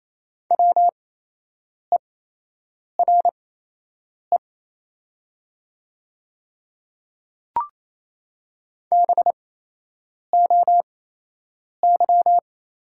Morse code, heard as a keyed tone, about 28 wpm, spaced as for 8 wpm, 700 Hz. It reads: WERE BOY
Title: 8x Character Spacing - Top 200 Words - 28wpm